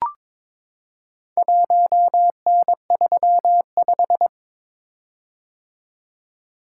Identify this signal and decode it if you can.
Morse code, heard as a keyed tone, about 22 wpm, 700 Hz. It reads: 1N35
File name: practice / 22wpm / Four Letter-Number Combinations - 22wpm